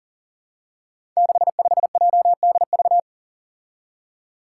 Morse code, heard as a keyed tone, about 40 wpm, 700 Hz. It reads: 65JDV
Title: Five Challenging Letter-Number Combinations 40wpm